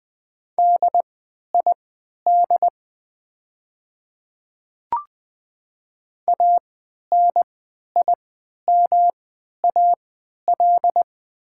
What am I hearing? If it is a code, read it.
Morse code, heard as a keyed tone, about 20 wpm, spaced as for 11 wpm, 700 Hz. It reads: DID ANIMAL
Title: 3x Character Spacing - Top 300 Words - 20wpm